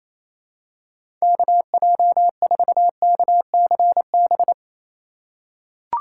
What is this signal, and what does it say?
Morse code, heard as a keyed tone, about 28 wpm, 700 Hz. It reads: KJ4KCB
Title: USA Call Signs - 6 Characters - 4x Character Spacing - 28wpm